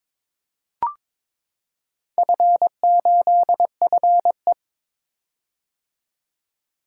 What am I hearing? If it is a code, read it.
Morse code, heard as a keyed tone, about 22 wpm, 700 Hz. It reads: F8FE
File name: practice / 22wpm / Four Letter-Number Combinations - 22wpm